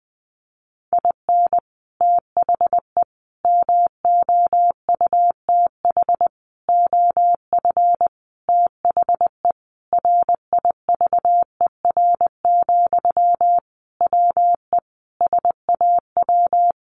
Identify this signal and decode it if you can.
Morse code, heard as a keyed tone, about 20 wpm, 700 Hz. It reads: IN THE MOUTH OF THE RIVER, WE SAW
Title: Sentences from Top 1500 Words 20wpm